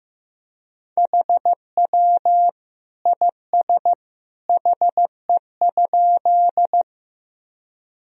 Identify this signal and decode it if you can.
Morse code, heard as a keyed tone, about 15 wpm, 700 Hz. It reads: HW IS HE?
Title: Sentences from Top 100 Words and Abbreviations 15wpm